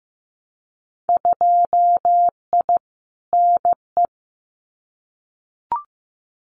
Morse code, heard as a keyed tone, about 15 wpm, 700 Hz. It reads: 2I NE
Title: Winter Field Day Exchanges Encore 15wpm